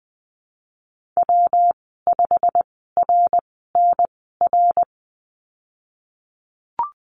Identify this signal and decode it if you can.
Morse code, heard as a keyed tone, about 19 wpm, spaced as for 14 wpm, 700 Hz. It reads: W5RNR
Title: 167 Lesson 27g - Number 7 - Call Signs 20wpm